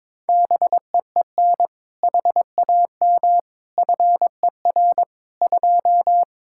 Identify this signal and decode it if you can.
Morse code, heard as a keyed tone, about 22 wpm, 700 Hz. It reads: BEEN HAM FER 2
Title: Practice QSO - Exchange 3 - More Info - 22wpm